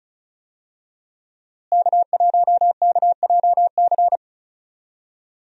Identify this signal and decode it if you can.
Morse code, heard as a keyed tone, about 36 wpm, 700 Hz. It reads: K1KJC
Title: USA Call Signs - 5 Characters - Repeated 3x - 35wpm